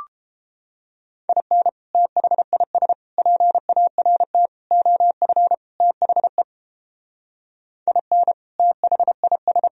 Morse code, heard as a keyed tone, about 29 wpm, 700 Hz. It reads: IN THIS PART OF THE IN THIS